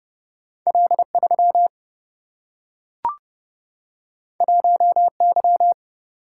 Morse code, heard as a keyed tone, about 30 wpm, 700 Hz. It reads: L 3 1 Y
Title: Single-Letter Single-Number 30wpm